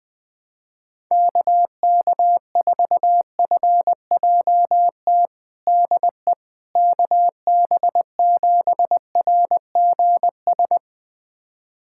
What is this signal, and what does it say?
Morse code, heard as a keyed tone, about 20 wpm, 700 Hz. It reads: KK4FJT DE KB7RGS
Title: USA Callsign DE Callsign - 6 Characters - 20wpm